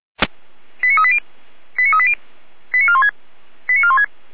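Five-tone selective call (ZVEI), heard as quick runs of five tones, tones of about 70 ms. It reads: 89380 89380 89416 89416
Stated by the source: ZVEI five-tone selective call received on an SDR, recorded by Feuerwehr Orlamünde fire brigade.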